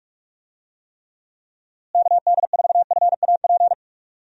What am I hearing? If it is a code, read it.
Morse code, heard as a keyed tone, about 46 wpm, 700 Hz. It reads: KD4FAP
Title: USA Call Signs - 6 Characters - Repeated 2x - 45wpm